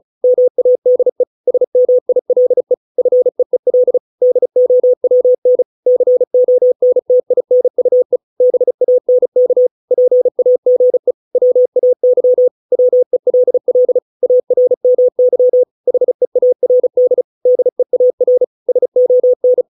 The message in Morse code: MADE SMILE FEEL DOWN CONTINUE BANK PAGE WAY WELL ARMY HEARD DEAR SON

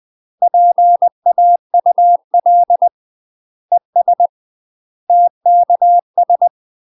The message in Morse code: PAUL ES TKS

